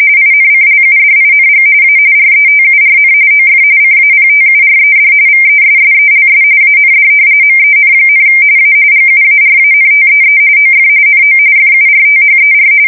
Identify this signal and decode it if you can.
RTTY, about 45.45 baud. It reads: INCLUDING A RETURN OF CAPTURED VESSELS AND MEN, A GUARANTEE OF NO FURTHER
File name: fsk170 / usnavy2